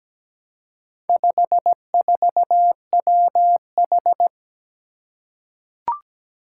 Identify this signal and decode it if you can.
Morse code, heard as a keyed tone, about 17 wpm, 700 Hz. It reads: 54WH
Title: Four Challenging Letter-Number Combinations 17wpm